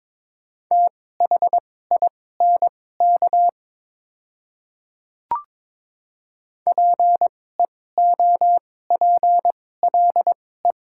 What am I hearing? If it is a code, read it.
Morse code, heard as a keyed tone, about 22 wpm, spaced as for 16 wpm, 700 Hz. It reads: THINK PEOPLE
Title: 2x Character Spacing - Top 200 Words - 22wpm